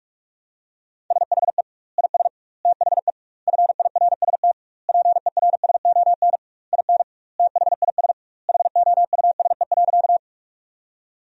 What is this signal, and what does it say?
Morse code, heard as a keyed tone, about 46 wpm, 700 Hz. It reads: SHE IS THE FIRST PERSON IN THIS HOUSE.
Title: 2x Word-Spacing - Sentences from Top 400 Words - 45wpm